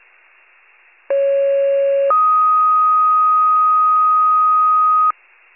Two-tone page, heard as a one-second tone, then a three-second tone, about 550 then 1250 Hz. It is heard from 1.1 to 5.1 s.
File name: wcalert